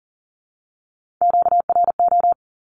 Morse code, heard as a keyed tone, about 40 wpm, 700 Hz. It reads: QRO
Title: Q-Codes 40wpm